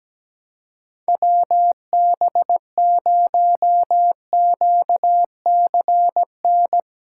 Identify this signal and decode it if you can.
Morse code, heard as a keyed tone, about 17 wpm, 700 Hz. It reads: WB0QCN
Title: USA Call Signs - 6 Characters - 6x Character Spacing - 17wpm